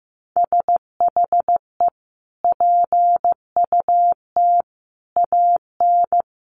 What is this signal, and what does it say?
Morse code, heard as a keyed tone, about 15 wpm, 700 Hz. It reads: SHE PUT AN
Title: Sentences from Top 1400 Words 15wpm